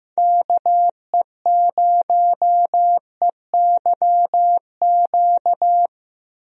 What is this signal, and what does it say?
Morse code, heard as a keyed tone, about 15 wpm, 700 Hz. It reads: KE0EYQ
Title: USA Call Signs - 6 Characters - Repeated 2x - 15wpm